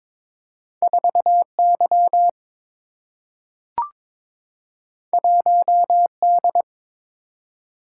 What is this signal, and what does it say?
Morse code, heard as a keyed tone, about 23 wpm, 700 Hz. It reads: 4 Y 1 D
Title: Single-Letter Single-Number 22wpm